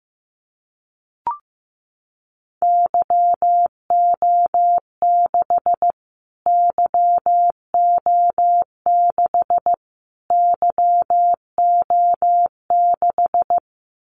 Morse code, heard as a keyed tone, about 15 wpm, 700 Hz. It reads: YO6 YO6 YO6
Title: Callsign Prefixes for Top 25 DXCC - Repeated 3x - 15wpm